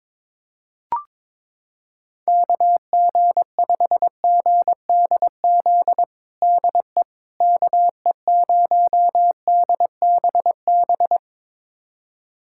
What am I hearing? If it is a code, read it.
Morse code, heard as a keyed tone, about 22 wpm, 700 Hz. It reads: KG5GDZ DE KE0DBB
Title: USA Callsign DE Callsign - 6 Characters - 22wpm